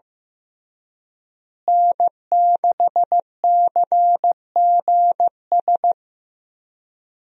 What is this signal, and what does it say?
Morse code, heard as a keyed tone, about 15 wpm, 700 Hz. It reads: N6CGS
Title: USA Call Signs - 5 Characters - Repeated 3x - 15wpm